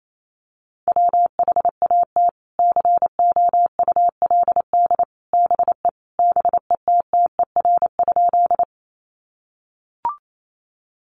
Morse code, heard as a keyed tone, about 28 wpm, 700 Hz. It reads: WHAT COULD BE BETTER?
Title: Sentences from Top 400 Words 28wpm